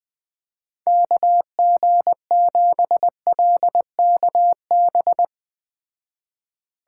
Morse code, heard as a keyed tone, about 20 wpm, 700 Hz. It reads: KG7LKB
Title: Speed Racing - USA 2x3 Call Signs 17wpm